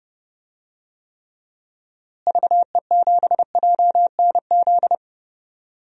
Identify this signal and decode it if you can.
Morse code, heard as a keyed tone, about 30 wpm, 700 Hz. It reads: VE7JNZ
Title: DXCC-100 Call Signs - 6 Characters - Repeated 2x - 30wpm